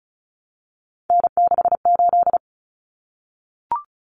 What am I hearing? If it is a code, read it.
Morse code, heard as a keyed tone, about 34 wpm, 700 Hz. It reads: N68
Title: Common Cut Numbers - 3-Digit 35wpm